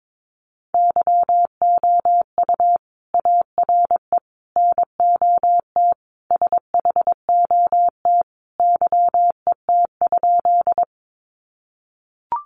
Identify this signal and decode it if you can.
Morse code, heard as a keyed tone, about 22 wpm, 700 Hz. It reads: YOU ARE NOT SHOT YET?
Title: Sentences from Top 1100 Words 22wpm